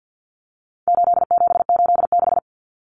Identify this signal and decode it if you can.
Morse code, heard as a keyed tone, about 50 wpm, 700 Hz. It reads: 8786
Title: Numbers - 4-Digit 50wpm